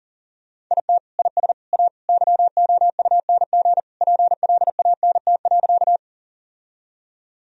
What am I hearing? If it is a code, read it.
Morse code, heard as a keyed tone, about 40 wpm, 700 Hz. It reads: IT IS A YOUNG PLANT.
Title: Sentences from Top 1000 Words - Review Part 1 - 40wpm